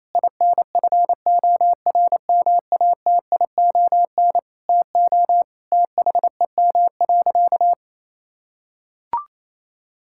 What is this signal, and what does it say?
Morse code, heard as a keyed tone, about 28 wpm, 700 Hz. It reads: INFORMATION TO THEM.
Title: Sentences from Top 1000 Words - Review Part 5 - 28wpm